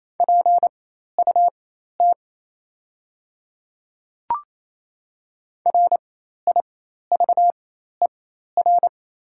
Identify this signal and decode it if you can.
Morse code, heard as a keyed tone, about 28 wpm, spaced as for 13 wpm, 700 Hz. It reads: PUT RIVER
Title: 4x Character Spacing - Top 300 Words - 28wpm